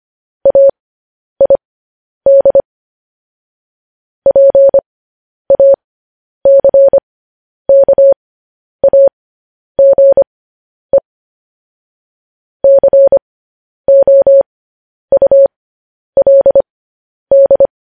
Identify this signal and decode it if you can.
Morse code, heard as a keyed tone, about 25 wpm, spaced as for 10 wpm, 550 Hz. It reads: AID PACKAGE COULD
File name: podcast / News headlines-1x-25x10